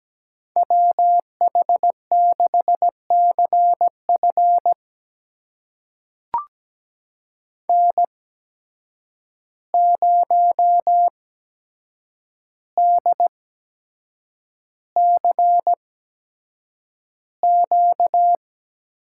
Morse code, heard as a keyed tone, about 17 wpm, spaced as for 7 wpm, 700 Hz. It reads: WH6CF N0DCQ
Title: USA Call Signs - 5 Characters - 8x Character Spacing - 17wpm